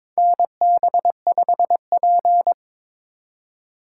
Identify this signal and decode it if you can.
Morse code, heard as a keyed tone, about 22 wpm, 700 Hz. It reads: NB5P
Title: USA Call Signs - 4 Characters - 2x Character Spacing - 22wpm